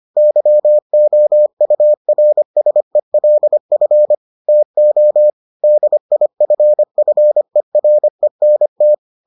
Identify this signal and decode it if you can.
Morse code, heard as a keyed tone, about 25 wpm, 600 Hz. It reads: YOURSELF TO DIFFERENT